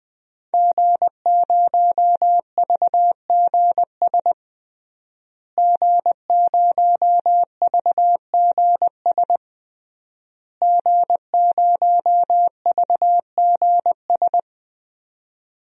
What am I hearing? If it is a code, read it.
Morse code, heard as a keyed tone, about 20 wpm, 700 Hz. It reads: G0VGS G0VGS G0VGS
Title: DXCC-100 Call Signs - 5 Characters - Repeated 3x - 20wpm